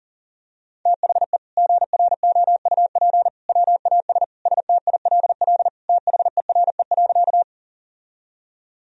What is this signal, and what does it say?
Morse code, heard as a keyed tone, about 40 wpm, 700 Hz. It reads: THE GROUP WAS STILL THERE.
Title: Sentences from Top 1000 Words - Review Part 1 - 40wpm